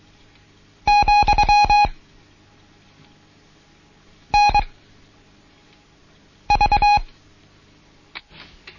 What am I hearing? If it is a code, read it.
Morse code, heard as a keyed tone, about 23 wpm, 830 Hz. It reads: , N V
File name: Morsen / Kurs A 16